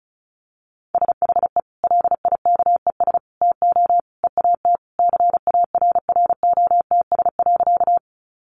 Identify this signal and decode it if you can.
Morse code, heard as a keyed tone, about 34 wpm, 700 Hz. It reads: SHE LIKES TO EAT CARROTS.
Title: Easy Phrases 35wpm